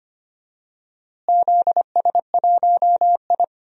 Morse code, heard as a keyed tone, about 25 wpm, 700 Hz. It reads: ZS1I
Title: DXCC-100 Call Signs - 4 Characters - Repeated 3x - 25wpm